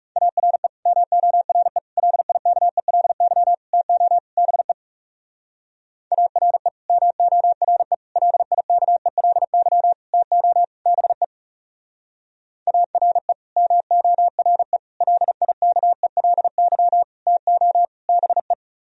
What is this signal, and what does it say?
Morse code, heard as a keyed tone, about 37 wpm, 700 Hz. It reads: ARE MORE LIKELY TO BE ARE MORE LIKELY TO BE ARE MORE LIKELY TO BE
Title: Speed-Racing - Sets of 5 Words 30wpm